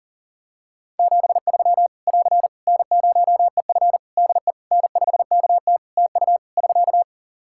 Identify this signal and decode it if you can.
Morse code, heard as a keyed tone, about 40 wpm, 700 Hz. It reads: 73 <AR> N0EF DE N5KT TU <SK>
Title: Practice QSO - Exchange 4 - Ending - 40wpm